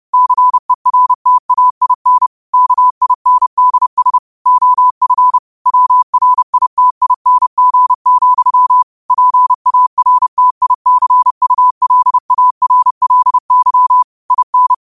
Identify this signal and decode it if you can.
Morse code, heard as a keyed tone, about 30 wpm, 1000 Hz. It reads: CERTAIN KINDS OF WRITING, PARTICULARLY IN
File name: qotdfiles / 30wpm